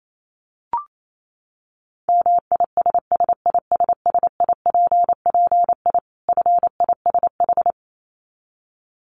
Morse code, heard as a keyed tone, about 28 wpm, 700 Hz. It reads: MISSISSIPPI FISH
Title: Dit Challenge 28wpm